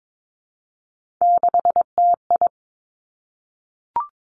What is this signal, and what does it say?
Morse code, heard as a keyed tone, about 22 wpm, 700 Hz. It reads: 6TI
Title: Three Challenging Letter-Number Combinations 22wpm